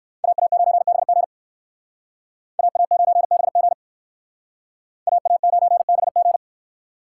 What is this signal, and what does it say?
Morse code, heard as a keyed tone, about 58 wpm, 700 Hz. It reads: AA9BG AA9BG AA9BG